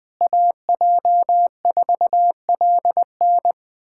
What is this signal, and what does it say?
Morse code, heard as a keyed tone, about 20 wpm, 700 Hz. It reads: AJ4LN
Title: K1USN SST QSO Practice - 20wpm